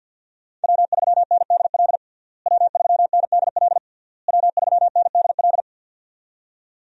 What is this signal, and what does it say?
Morse code, heard as a keyed tone, about 52 wpm, 700 Hz. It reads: W3NDL W3NDL W3NDL
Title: USA Call Signs - 5 Characters - Repeated 3x - 50wpm